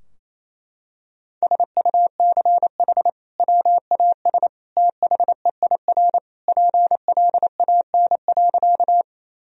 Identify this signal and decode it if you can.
Morse code, heard as a keyed tone, about 27 wpm, 700 Hz. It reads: SUCH WAS THEIR PLAN.
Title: Sentences from Top 1000 Words - Review Part 2 - 28wpm